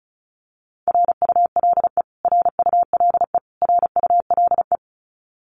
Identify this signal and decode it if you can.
Morse code, heard as a keyed tone, about 34 wpm, 700 Hz. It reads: RULE RULE RULE
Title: Top 500 Words Repeated 35wpm